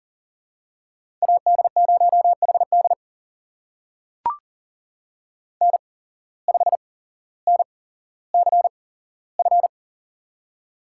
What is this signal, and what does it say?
Morse code, heard as a keyed tone, about 40 wpm, spaced as for 17 wpm, 700 Hz. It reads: AD0HD N5NCF
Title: USA Call Signs - 8x Character Spacing - 40wpm